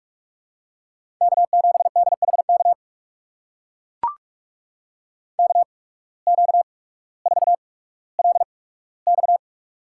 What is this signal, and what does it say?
Morse code, heard as a keyed tone, about 44 wpm, spaced as for 19 wpm, 700 Hz. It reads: K7DHK KQ4LX
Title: USA Call Signs - 5 Characters - 8x Character Spacing - 45wpm